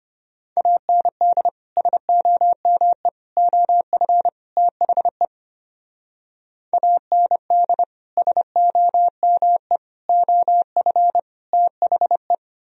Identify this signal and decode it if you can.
Morse code, heard as a keyed tone, about 28 wpm, 690 Hz. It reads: AND SOME OF THE AND SOME OF THE
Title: Speed-Racing - Sets of 4 Words 25wpm